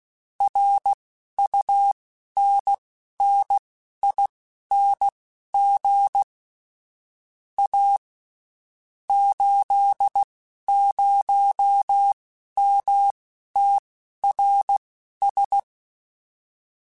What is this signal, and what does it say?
Morse code, heard as a keyed tone, about 16 wpm, spaced as for 11 wpm, 800 Hz. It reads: RUNNING A 80MTRS
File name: QSOs / QSOs-mp3s / WD0FBT 1@15